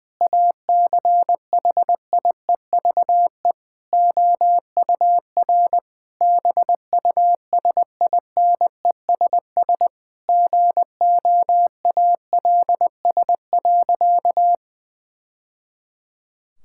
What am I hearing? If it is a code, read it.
Morse code, heard as a keyed tone, about 20 wpm, 700 Hz. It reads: ACHIEVE OUR BUSINESS GOALS.
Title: Sentences from Top 1000 Words - Review Part 6 - 20wpm